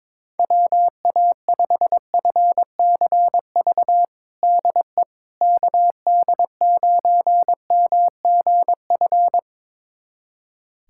Morse code, heard as a keyed tone, about 22 wpm, 700 Hz. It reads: WA5FCV DE KD9MGF
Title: USA Callsign DE Callsign - 6 Characters - 22wpm